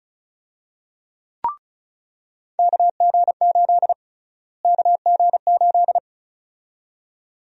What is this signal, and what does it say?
Morse code, heard as a keyed tone, about 36 wpm, 700 Hz. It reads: KG8 KG8
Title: USA Call Sign Prefixes - Repeated 2x - 35wpm